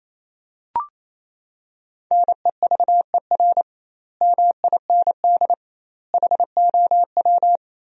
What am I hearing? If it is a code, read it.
Morse code, heard as a keyed tone, about 28 wpm, 700 Hz. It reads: NEVER MIND HOW
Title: 2x Word-Spacing - Sentences from Top 500 Words - 28wpm